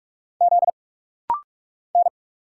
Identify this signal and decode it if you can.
Morse code, heard as a keyed tone, about 46 wpm, 700 Hz. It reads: Z N
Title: Single Letter-Number Alt - Rapid-Fire 45wpm